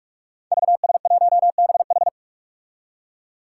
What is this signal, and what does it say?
Morse code, heard as a keyed tone, about 46 wpm, 700 Hz. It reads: VS1BH
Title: Five Challenging Letter-Number Combinations 45wpm